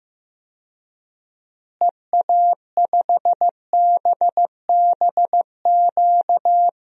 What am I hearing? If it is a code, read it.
Morse code, heard as a keyed tone, about 16 wpm, 700 Hz. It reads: EA5BBQ
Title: DXCC-100 Call Signs - 6 Characters - Repeated 2x - 15wpm